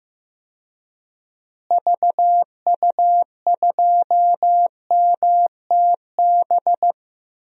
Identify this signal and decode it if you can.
Morse code, heard as a keyed tone, about 15 wpm, 700 Hz. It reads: VU2MTB
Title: DXCC-100 Call Signs - 6 Characters - Repeated 3x - 15wpm